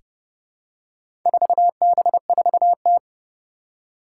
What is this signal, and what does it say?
Morse code, heard as a keyed tone, about 30 wpm, 700 Hz. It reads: 4B4T
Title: Four Challenging Letter-Number Combinations 30wpm